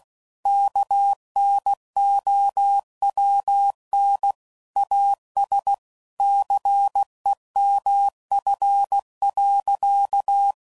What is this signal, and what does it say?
Morse code, heard as a keyed tone, about 16 wpm, 800 Hz. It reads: KNOWN AS CEMF.
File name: CW Code Groups @15WPM / Technical Subjects / XL